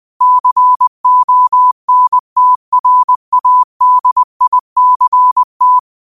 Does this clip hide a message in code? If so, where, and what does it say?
Yes: Morse code from 0.2 to 5.8 s, heard as a keyed tone, about 20 wpm, 1000 Hz. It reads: CONTRADICT